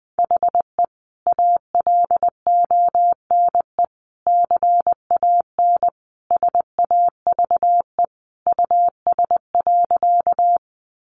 Morse code, heard as a keyed tone, about 20 wpm, 700 Hz. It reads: HE ALONE CAN SAVE US.
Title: Sentences from Top 1100 Words 20wpm